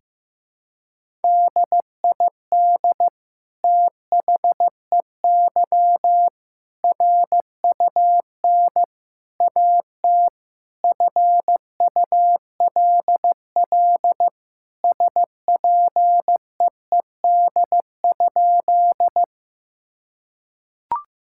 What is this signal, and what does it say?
Morse code, heard as a keyed tone, about 15 wpm, 700 Hz. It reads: DID THEY RUN AT FULL SPEED?
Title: Sentences from Top 1000 Words - Review Part 5 - 15wpm